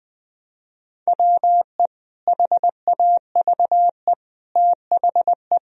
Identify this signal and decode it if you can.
Morse code, heard as a keyed tone, about 21 wpm, 700 Hz. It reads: WE HAVE THE
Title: Sentences from Top 1000 Words - Review Part 3 - 20wpm